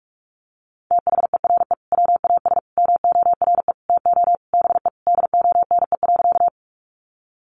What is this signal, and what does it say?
Morse code, heard as a keyed tone, about 44 wpm, 700 Hz. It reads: THERE WAS MORE TO BE DONE.
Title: Sentences from Top 500 Words 45wpm